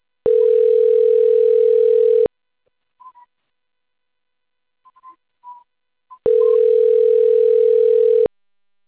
Ringback tone (North America), 440 + 480 Hz.
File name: rg-1600-unknown- - - .10